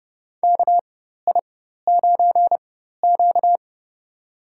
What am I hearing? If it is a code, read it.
Morse code, heard as a keyed tone, about 30 wpm, spaced as for 14 wpm, 700 Hz. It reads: KI9Q
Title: USA Call Signs - 4 Characters - 4x Character Spacing - 30wpm